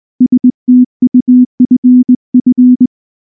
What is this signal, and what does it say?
Morse code, heard as a keyed tone, about 19 wpm, 250 Hz. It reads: STUFF